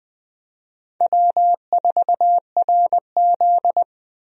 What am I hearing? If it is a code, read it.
Morse code, heard as a keyed tone, about 20 wpm, 700 Hz. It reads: W4RZ
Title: USA Call Signs - 4 Characters - Repeated 3x - 20wpm